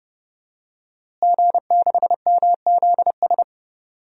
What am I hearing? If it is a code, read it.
Morse code, heard as a keyed tone, about 31 wpm, 700 Hz. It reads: G6MZS
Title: DXCC-100 Call Signs - 5 Characters - Repeated 2x - 30wpm